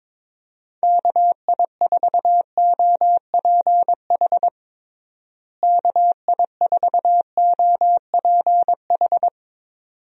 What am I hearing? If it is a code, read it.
Morse code, heard as a keyed tone, about 22 wpm, 700 Hz. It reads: KI4OPH KI4OPH
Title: USA Call Signs - 6 Characters - Repeated 2x - 22wpm